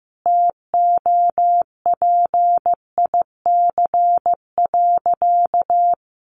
Morse code, heard as a keyed tone, about 15 wpm, 700 Hz. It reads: TOPIC.
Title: Sentences from Top 1200 Words 15wpm